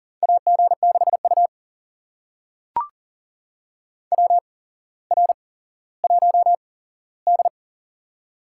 Morse code, heard as a keyed tone, about 44 wpm, spaced as for 17 wpm, 700 Hz. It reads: AG6U WR1D
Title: USA Call Signs - 4 Characters - 8x Character Spacing - 40wpm